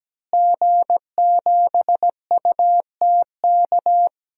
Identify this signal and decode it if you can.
Morse code, heard as a keyed tone, about 17 wpm, 700 Hz. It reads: G7UTK
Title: DXCC-100 Call Signs - 5 Characters - Repeated 2x - 17wpm